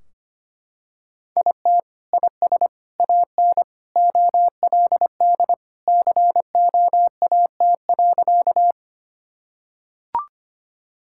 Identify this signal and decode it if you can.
Morse code, heard as a keyed tone, about 25 wpm, 700 Hz. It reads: IT IS AN OLD COAT.
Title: Sentences from Top 1600 Words 25wpm